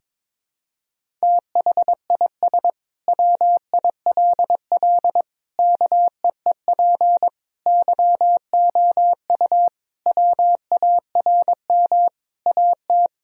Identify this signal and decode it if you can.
Morse code, heard as a keyed tone, about 22 wpm, 700 Hz. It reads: THIS WILL KEEP YOU WARM AT
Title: Sentences from Top 1000 Words - Review Part 2 - 22wpm